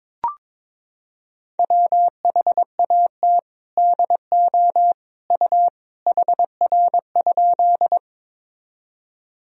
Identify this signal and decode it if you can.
Morse code, heard as a keyed tone, about 21 wpm, 700 Hz. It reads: WHAT DO U HR?
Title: Sentences from Top 300 Words and Abbreviations 22wpm